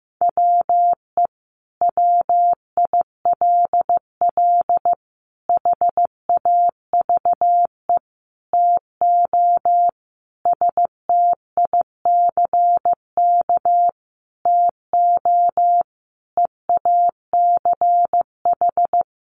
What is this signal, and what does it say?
Morse code, heard as a keyed tone, about 15 wpm, 700 Hz. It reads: WE WILL HAVE TO STICK TO EACH
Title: Sentences from Top 1200 Words 15wpm